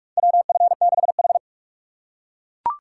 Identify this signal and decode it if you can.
Morse code, heard as a keyed tone, about 42 wpm, 700 Hz. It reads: WF6H
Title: USA Call Signs - 3x Character Spacing - 45wpm